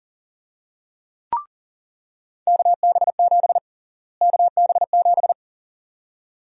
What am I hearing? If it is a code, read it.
Morse code, heard as a keyed tone, about 40 wpm, 700 Hz. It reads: KB7 KB7
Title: USA Call Sign Prefixes - Repeated 2x - 40wpm